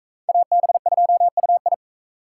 Morse code, heard as a keyed tone, about 43 wpm, 700 Hz. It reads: AB2UI